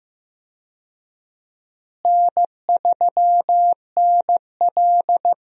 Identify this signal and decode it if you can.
Morse code, heard as a keyed tone, about 15 wpm, 700 Hz. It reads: N3NL
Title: USA Call Signs - 4 Characters - Repeated 2x - 15wpm